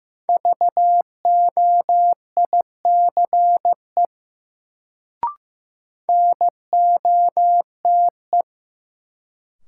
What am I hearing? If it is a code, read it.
Morse code, heard as a keyed tone, about 15 wpm, 700 Hz. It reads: VOICE NOTE
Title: Sentences from Top 500 Words - New Word Drill - 15wpm